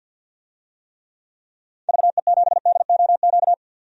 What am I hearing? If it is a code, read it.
Morse code, heard as a keyed tone, about 53 wpm, 700 Hz. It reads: VE7DOQ
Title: DXCC-100 Call Signs - 6 Characters - 50wpm